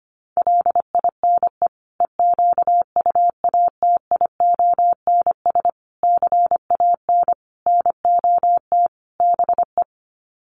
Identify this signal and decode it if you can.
Morse code, heard as a keyed tone, about 25 wpm, 700 Hz. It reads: LINE EQUATIONS CAN NOT BE
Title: Sets of 5 Words 25wpm